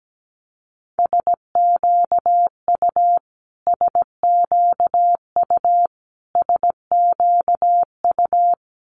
Morse code, heard as a keyed tone, about 17 wpm, 700 Hz. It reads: SQU SQU SQU
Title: Multi-letter Phonemes Repeated 3x 17wpm